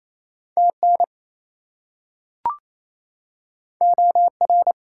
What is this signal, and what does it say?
Morse code, heard as a keyed tone, about 28 wpm, 700 Hz. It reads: TN OR
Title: US States and Canadian Provinces 28wpm